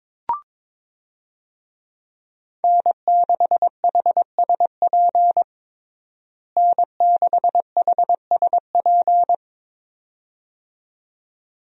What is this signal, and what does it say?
Morse code, heard as a keyed tone, about 21 wpm, 700 Hz. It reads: N6HSP N6HSP
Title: USA Call Signs - 5 Characters - Repeated 2x - 22wpm